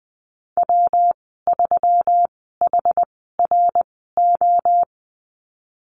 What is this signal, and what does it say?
Morse code, heard as a keyed tone, about 20 wpm, spaced as for 15 wpm, 700 Hz. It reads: W3HRO
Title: 122 Lesson 20g - Number 3 - Call Signs 20wpm